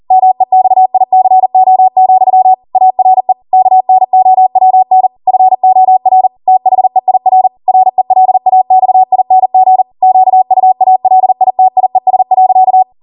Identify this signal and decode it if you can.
Morse code, heard as a keyed tone, about 40 wpm, 750 Hz. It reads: MEXICO, ARE KNOWN FOR THEIR RELAXING QUALITIES.